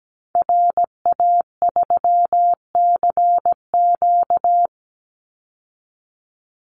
Morse code, heard as a keyed tone, about 17 wpm, 700 Hz. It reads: RA3CQ
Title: Call Signs Repeated 3x - 17wpm